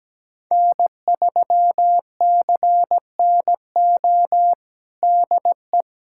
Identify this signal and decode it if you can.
Morse code, heard as a keyed tone, about 17 wpm, 700 Hz. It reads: N3CNO DE
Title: USA Callsign DE Callsign 17wpm